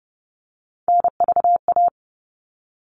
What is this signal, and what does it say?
Morse code, heard as a keyed tone, about 29 wpm, 700 Hz. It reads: NVA
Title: All Cut Numbers - 3-Digit 30wpm